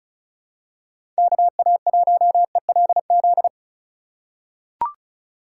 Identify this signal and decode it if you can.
Morse code, heard as a keyed tone, about 36 wpm, 700 Hz. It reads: KA1ELZ
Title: USA Call Signs - 6 Characters - 3x Character Spacing - 35wpm